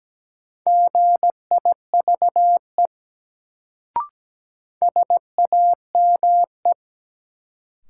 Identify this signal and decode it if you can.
Morse code, heard as a keyed tone, about 17 wpm, 700 Hz. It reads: GIVE SAME
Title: Sentences from Top 200 Words - New Word Drill - 17wpm